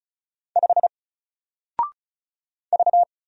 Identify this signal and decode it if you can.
Morse code, heard as a keyed tone, about 35 wpm, 700 Hz. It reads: 5 V
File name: Single-Letter-Number-35wpm